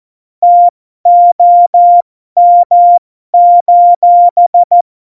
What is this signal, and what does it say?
Morse code, heard as a keyed tone, about 14 wpm, spaced as for 12 wpm, 700 Hz. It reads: TOM:
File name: Morsekurs / Witze